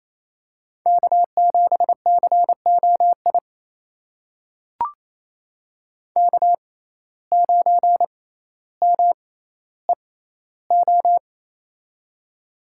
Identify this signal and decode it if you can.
Morse code, heard as a keyed tone, about 28 wpm, spaced as for 14 wpm, 700 Hz. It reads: K7COI K9MEO
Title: USA Call Signs - 5 Characters - 6x Character Spacing - 28wpm